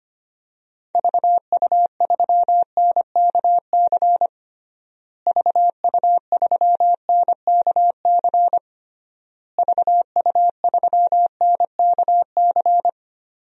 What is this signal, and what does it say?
Morse code, heard as a keyed tone, about 25 wpm, 700 Hz. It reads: VU3NKC VU3NKC VU3NKC
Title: DXCC-100 Call Signs - 6 Characters - Repeated 3x - 25wpm